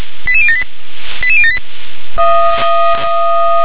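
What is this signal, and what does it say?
Five-tone selective call (ZVEI), heard as quick runs of five tones, tones of about 70 ms. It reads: 88078 88078
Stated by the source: ZVEI five-tone selective call received by SDR.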